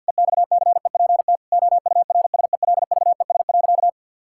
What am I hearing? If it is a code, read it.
Morse code, heard as a keyed tone, about 50 wpm, 700 Hz. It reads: EXCEPT OURSELVES.
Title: Sentences from Top 1600 Words 50wpm